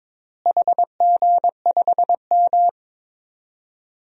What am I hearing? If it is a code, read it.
Morse code, heard as a keyed tone, about 22 wpm, 700 Hz. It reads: HG5M
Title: DXCC-100 Call Signs - 4 Characters - Repeated 2x - 22wpm